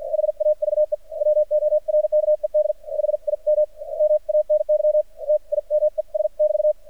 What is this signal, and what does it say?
Morse code, heard as a keyed tone, about 50 wpm, spaced as for 42 wpm, 610 Hz. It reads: HAVE WORKED HIM MANY TIMES<BT>
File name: archived / CRSnet